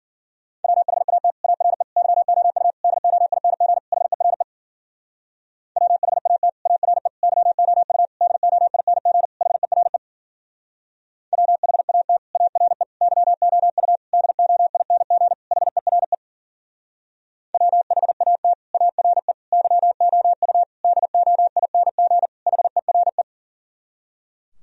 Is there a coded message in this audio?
Yes: Morse code from 0.6 to 23.2 s, heard as a keyed tone, about 51 wpm, 700 Hz. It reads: WHAT ARE YOU DOING HERE WHAT ARE YOU DOING HERE WHAT ARE YOU DOING HERE WHAT ARE YOU DOING HERE